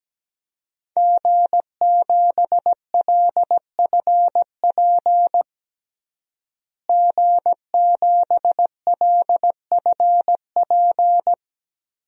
Morse code, heard as a keyed tone, about 17 wpm, 700 Hz. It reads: G7LFP G7LFP
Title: DXCC-100 Call Signs - 5 Characters - Repeated 3x - 17wpm